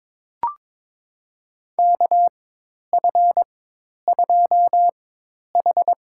Morse code, heard as a keyed tone, about 22 wpm, spaced as for 10 wpm, 700 Hz. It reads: KF2H
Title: USA Call Signs - 4 Characters - 4x Character Spacing - 22wpm